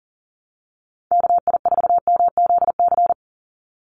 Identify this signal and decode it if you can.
Morse code, heard as a keyed tone, about 40 wpm, 700 Hz. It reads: KI4MZC
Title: Call Signs Repeated 2x - 40wpm